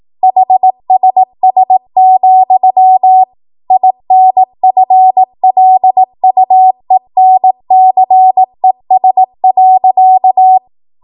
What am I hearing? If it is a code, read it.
Morse code, heard as a keyed tone, about 18 wpm, 750 Hz. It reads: HSS, INFLUENCES.
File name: Morse / 18WPM